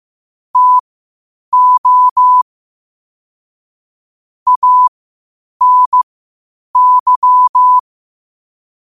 Morse code, heard as a keyed tone, about 15 wpm, spaced as for 8 wpm, 1000 Hz. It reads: TO ANY
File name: 8wpm